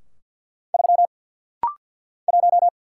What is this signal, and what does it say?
Morse code, heard as a keyed tone, about 46 wpm, 700 Hz. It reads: 3 1
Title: Single Letter-Number - Rapid-Fire 50wpm